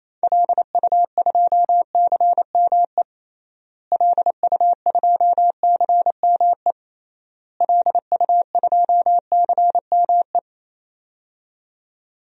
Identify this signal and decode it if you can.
Morse code, heard as a keyed tone, about 28 wpm, 700 Hz. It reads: LU2CME LU2CME LU2CME
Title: DXCC-100 Call Signs - 6 Characters - Repeated 3x - 28wpm